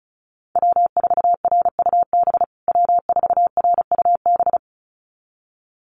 Morse code, heard as a keyed tone, about 34 wpm, 700 Hz. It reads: W4RUB W4RUB
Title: USA Call Signs Repeated 2x 35wpm